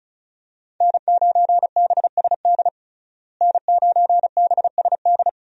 Morse code, heard as a keyed tone, about 34 wpm, 700 Hz. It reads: N9BSD N9BSD
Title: USA Call Signs - 5 Characters - Repeated 2x - 35wpm